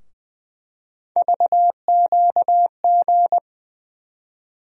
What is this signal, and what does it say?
Morse code, heard as a keyed tone, about 21 wpm, 700 Hz. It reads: VQG
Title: USA Call Sign Postfixes - 3 Letters 20wpm